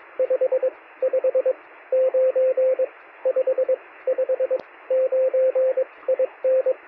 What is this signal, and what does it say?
Morse code, heard as a keyed tone, about 22 wpm, spaced as for 16 wpm, 520 Hz, with radio noise. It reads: 559 559 IN
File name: sm7lcb / 2114z oh3lwp 13cm